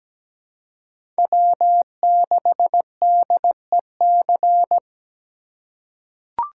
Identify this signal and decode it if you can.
Morse code, heard as a keyed tone, about 17 wpm, 700 Hz. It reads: W6DEC
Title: USA Call Signs - 5 Characters - 2x Character Spacing - 17wpm